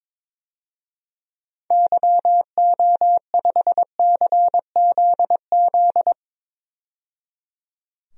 Morse code, heard as a keyed tone, about 22 wpm, 700 Hz. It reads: YO5CZZ
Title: DXCC-100 Call Signs - 6 Characters - 22wpm